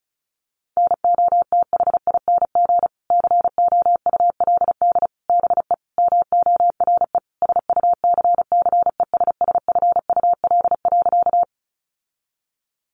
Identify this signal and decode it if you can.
Morse code, heard as a keyed tone, about 34 wpm, 700 Hz. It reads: NOTHING COULD BE MORE SUCCESSFUL.
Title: Sentences from Top 900 Words 35wpm